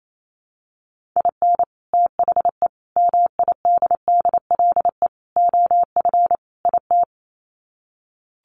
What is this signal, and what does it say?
Morse code, heard as a keyed tone, about 28 wpm, 700 Hz. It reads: IN THE MIDDLE OF IT
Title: Sets of 5 Words Encore 28wpm